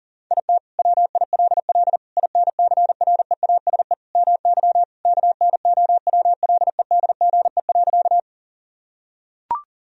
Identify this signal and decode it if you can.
Morse code, heard as a keyed tone, about 40 wpm, 700 Hz. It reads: IT WILL INCREASE MY KNOWLEDGE.
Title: Sentences from Top 1000 Words - Review Part 4 - 40wpm